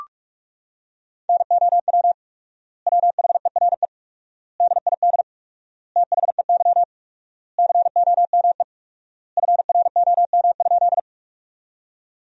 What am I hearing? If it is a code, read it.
Morse code, heard as a keyed tone, about 41 wpm, 700 Hz. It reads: NOW WHERE DID THEY COME FROM?
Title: 4x Word-Spacing - Sentences from Top 200 Words - 45wpm